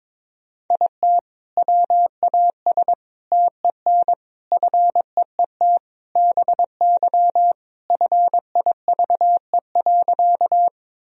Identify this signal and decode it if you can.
Morse code, heard as a keyed tone, about 21 wpm, 700 Hz. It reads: IT WAS TEN FEET BY FIVE.
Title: Sentences from Top 1000 Words - Review Part 2 - 22wpm